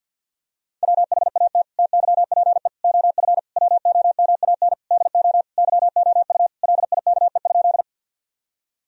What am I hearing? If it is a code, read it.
Morse code, heard as a keyed tone, about 50 wpm, 700 Hz. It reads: WHAT TYPE OF WOMAN DO YOU LIKE?
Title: Sentences from Top 1000 Words - Review Part 3 - 50wpm